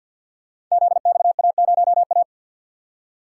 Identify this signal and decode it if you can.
Morse code, heard as a keyed tone, about 48 wpm, 700 Hz. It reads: ZXA0A
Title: Code Groups - 5 Letter:Numbers - 50wpm